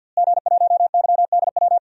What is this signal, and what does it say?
Morse code, heard as a keyed tone, about 52 wpm, 700 Hz. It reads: G1YDW
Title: DXCC-100 Call Signs - 5 Characters - Repeated 2x - 50wpm